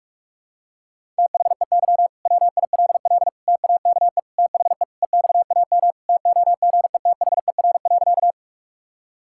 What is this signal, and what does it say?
Morse code, heard as a keyed tone, about 47 wpm, 700 Hz. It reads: THEY WILL TAKE THE EXAM TOGETHER.
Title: Sentences from Top 1000 Words - Review Part 5 - 45wpm